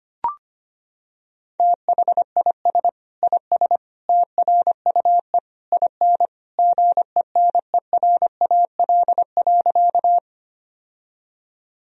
Morse code, heard as a keyed tone, about 25 wpm, 700 Hz. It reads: THIS IS TRUE IN GENERAL.
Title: Sentences from Top 1000 Words - Review Part 3 - 25wpm